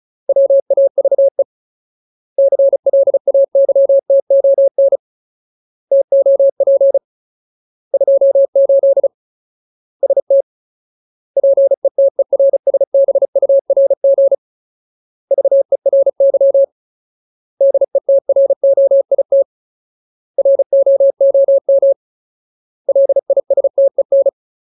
Morse code, heard as a keyed tone, about 36 wpm, 550 Hz. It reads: WAVE CLAYTON TOP 28 ST PETERSBURG VERY DETROIT ROOM LISTEN
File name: Ultimate Word List / 35WPM Ultimate